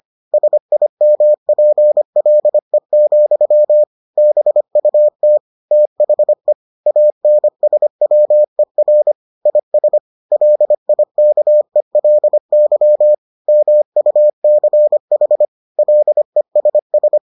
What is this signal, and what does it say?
Morse code, heard as a keyed tone, about 25 wpm, 600 Hz. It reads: SIMPLE, BUT THE ANSWER IS LIKELY MUCH LESS